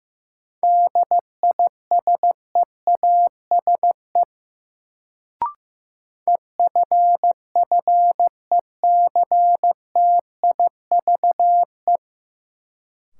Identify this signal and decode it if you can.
Morse code, heard as a keyed tone, about 15 wpm, 700 Hz. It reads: DISEASE EFFECTIVE
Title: Sentences from Top 1000 Words - Review Part 6 - New Word Drill - 15wpm